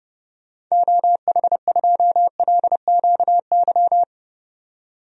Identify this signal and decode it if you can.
Morse code, heard as a keyed tone, about 30 wpm, 700 Hz. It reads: OH2LQY
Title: DXCC-100 Call Signs - 6 Characters - 30wpm